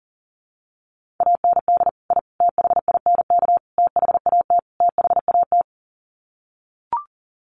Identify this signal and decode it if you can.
Morse code, heard as a keyed tone, about 40 wpm, 700 Hz. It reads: AND I THINK THAT THAT
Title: Sets of 5 Words Encore 40wpm